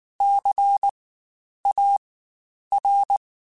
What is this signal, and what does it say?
Morse code, heard as a keyed tone, about 19 wpm, spaced as for 9 wpm, 800 Hz. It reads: CAR
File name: Spaced Alphas / Spaced Alphas mp3s / Cs sp @18